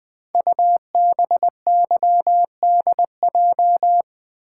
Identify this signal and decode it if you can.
Morse code, heard as a keyed tone, about 20 wpm, 700 Hz. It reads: UBYDJ
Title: Code Groups - 5 Letters - 20wpm